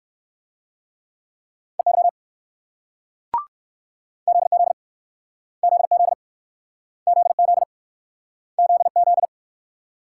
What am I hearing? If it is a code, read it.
Morse code, heard as a keyed tone, about 64 wpm, 700 Hz. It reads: EY ZZ ZZ ZZ ZZ